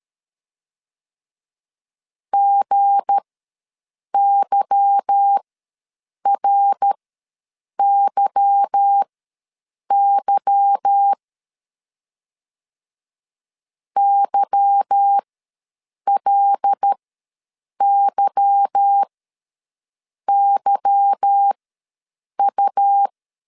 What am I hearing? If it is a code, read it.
Morse code, heard as a keyed tone, about 13 wpm, spaced as for 7 wpm, 800 Hz. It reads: GYRYY YLYYU
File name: LEZIONI / 8 YJ45 / 8 Y ws 12wpm 3409